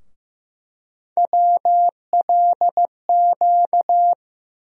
Morse code, heard as a keyed tone, about 15 wpm, 700 Hz. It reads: WLQ
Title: USA Call Sign Postfixes - 3 Letters 15wpm